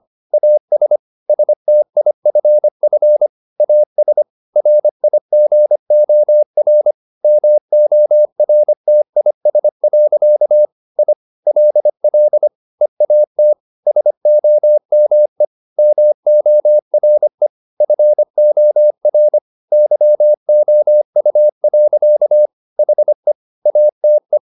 Morse code, heard as a keyed tone, about 25 wpm, 600 Hz. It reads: AS STIFF AS RIGOR MORTIS. I LL EAT SOME MORE FOR YOU. HE ATE
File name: the old man and the sea part 1 25wpm